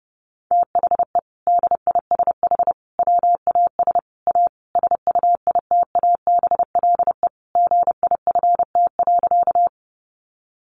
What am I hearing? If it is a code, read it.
Morse code, heard as a keyed tone, about 30 wpm, 700 Hz. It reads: THE DISH WAS A SUITABLE GIFT.
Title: Sentences from Top 1400 Words 30wpm